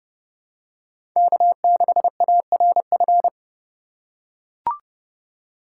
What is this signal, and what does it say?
Morse code, heard as a keyed tone, about 31 wpm, 700 Hz. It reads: K6ARF
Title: USA Call Signs - 3x Character Spacing - 30wpm